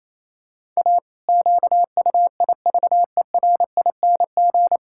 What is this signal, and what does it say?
Morse code, heard as a keyed tone, about 27 wpm, 700 Hz. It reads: A QUIVERING
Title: Pangram Challenge v2 - 28wpm